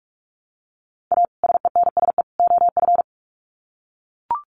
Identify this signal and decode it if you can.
Morse code, heard as a keyed tone, about 44 wpm, 700 Hz. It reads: A SENSE OF
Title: Sets of 3 Words 45wpm